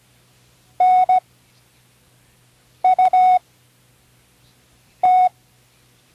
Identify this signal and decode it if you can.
Morse code, heard as a keyed tone, about 17 wpm, spaced as for 5 wpm, 740 Hz. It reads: NUT
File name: CW Learning Programs / Learning Morse Code (from Tune in the World) / 08 words